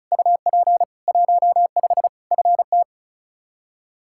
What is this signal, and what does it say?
Morse code, heard as a keyed tone, about 36 wpm, 700 Hz. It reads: UP 15 FT
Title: K1USN SST QSO Practice 35wpm